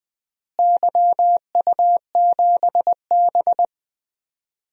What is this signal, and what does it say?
Morse code, heard as a keyed tone, about 21 wpm, 700 Hz. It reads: YU7B
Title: DXCC-100 Call Signs - 4 Characters - Repeated 3x - 20wpm